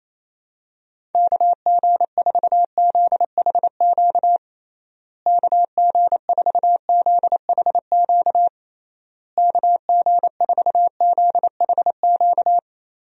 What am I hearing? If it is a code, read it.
Morse code, heard as a keyed tone, about 28 wpm, 700 Hz. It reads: KG4ZHQ KG4ZHQ KG4ZHQ
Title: USA Call Signs - 6 Characters - Repeated 3x - 28wpm